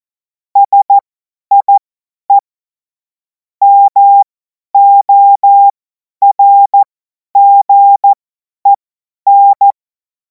Morse code, heard as a keyed tone, about 14 wpm, spaced as for 10 wpm, 800 Hz. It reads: SIE MORGEN
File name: Morsekurs / Witze / Witz 04. August 2026 50 F70